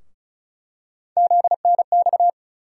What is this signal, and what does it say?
Morse code, heard as a keyed tone, about 36 wpm, 700 Hz. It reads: ZNX
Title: USA Call Sign Postfixes - 3 Letters 35wpm